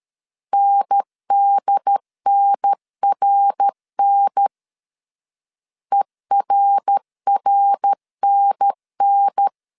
Morse code, heard as a keyed tone, about 13 wpm, 800 Hz. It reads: NDNRN ERRNN
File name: CW / DET / 1 3 1500C S B 12wpm 2828